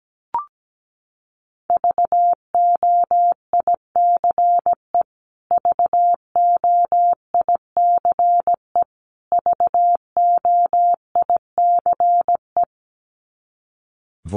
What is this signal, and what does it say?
Morse code, heard as a keyed tone, about 17 wpm, 700 Hz. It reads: VOICE VOICE VOICE
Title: Top 500 Words Repeated 17wpm